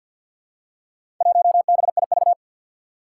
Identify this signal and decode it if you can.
Morse code, heard as a keyed tone, about 51 wpm, 700 Hz. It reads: 1BIV
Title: Four Challenging Letter-Number Combinations 50wpm